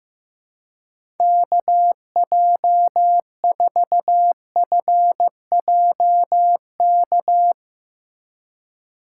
Morse code, heard as a keyed tone, about 15 wpm, 700 Hz. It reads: KJ4FJK
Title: USA Call Signs - 6 Characters - 3x Character Spacing - 15wpm